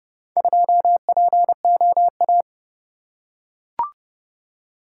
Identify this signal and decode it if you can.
Morse code, heard as a keyed tone, about 29 wpm, 700 Hz. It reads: 2POA